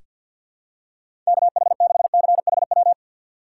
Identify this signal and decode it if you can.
Morse code, heard as a keyed tone, about 48 wpm, 700 Hz. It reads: KH6CHW
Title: USA Call Signs - 6 Characters - 4x Character Spacing - 50wpm